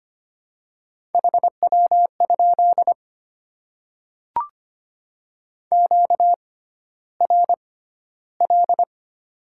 Morse code, heard as a keyed tone, about 25 wpm, spaced as for 13 wpm, 700 Hz. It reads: HW? QRL
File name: Top 20 QSO Elements - 6x Character Spacing - 25wpm